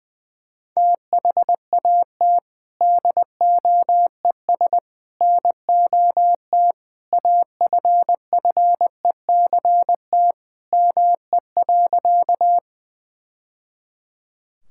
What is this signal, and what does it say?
Morse code, heard as a keyed tone, about 20 wpm, 700 Hz. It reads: THAT DOES NOT AFFECT ME.
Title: Sentences from Top 1000 Words - Review Part 6 - 20wpm